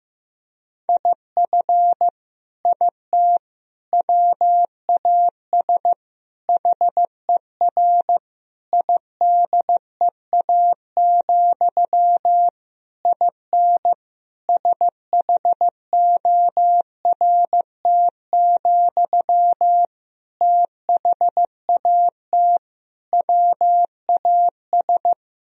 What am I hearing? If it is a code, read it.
Morse code, heard as a keyed tone, about 15 wpm, 700 Hz. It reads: IF IT WAS HER IDEA, IN SHORT, THAT WAS
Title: Sentences from Top 1000 Words - Review Part 2 - 15wpm